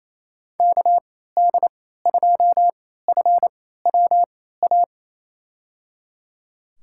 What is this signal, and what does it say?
Morse code, heard as a keyed tone, about 28 wpm, spaced as for 16 wpm, 700 Hz. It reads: KD2FWA
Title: USA Call Signs - 6 Characters - 3x Character Spacing - 28wpm